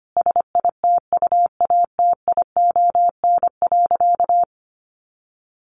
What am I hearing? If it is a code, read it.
Morse code, heard as a keyed tone, about 25 wpm, 700 Hz. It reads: SITUATION.
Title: Sentences from Top 1200 Words 25wpm